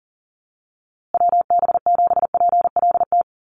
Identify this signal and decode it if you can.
Morse code, heard as a keyed tone, about 40 wpm, 700 Hz. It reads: WB7PLT
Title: USA Call Signs Repeated 3x 40wpm